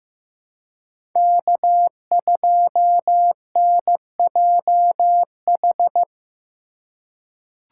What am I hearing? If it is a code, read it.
Morse code, heard as a keyed tone, about 15 wpm, 700 Hz. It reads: K2NJH
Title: USA Call Signs - 2x Character Spacing - 15wpm